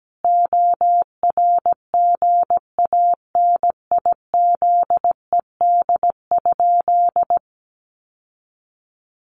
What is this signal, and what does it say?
Morse code, heard as a keyed tone, about 17 wpm, 700 Hz. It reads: ORGANIZED?
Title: Sentences from Top 1300 Words 17wpm